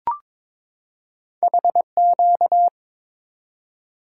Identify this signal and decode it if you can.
Morse code, heard as a keyed tone, about 22 wpm, 700 Hz. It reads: HQ
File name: practice / 22wpm / Two Letter-Number Combinations - 22wpm